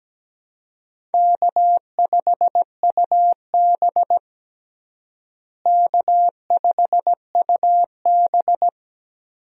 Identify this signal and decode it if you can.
Morse code, heard as a keyed tone, about 17 wpm, 700 Hz. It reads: K5UB K5UB
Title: USA Call Signs - 4 Characters - Repeated 2x - 17wpm